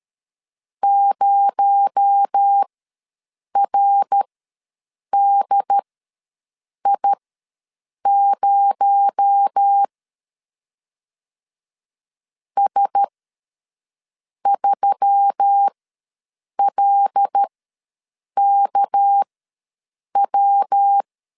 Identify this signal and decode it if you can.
Morse code, heard as a keyed tone, about 13 wpm, spaced as for 6 wpm, 800 Hz. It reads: 0RDI0 S3LKW
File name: LEZIONI / 6 PKX0 / 6 PKX0 ws 12wpm 3448